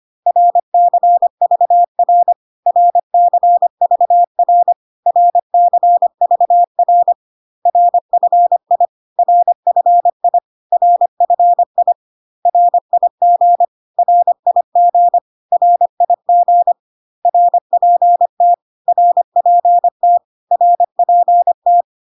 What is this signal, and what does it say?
Morse code, heard as a keyed tone, about 25 wpm, 700 Hz. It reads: RCVR RCVR RCVR RFI RFI RFI RIG RIG RIG RPT RPT RPT
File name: CW abbreviations alphabetical repeated 3 times 25 25wpm ik1hge